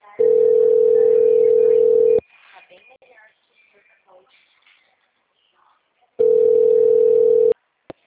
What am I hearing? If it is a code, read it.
Ringback tone (North America), 440 + 480 Hz, with speech.
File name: out- -2011- - - .541